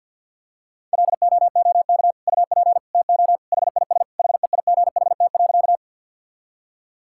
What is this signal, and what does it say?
Morse code, heard as a keyed tone, about 50 wpm, 700 Hz. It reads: LOOK UP TO HIS HEIGHT.
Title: Sentences from Top 1600 Words 50wpm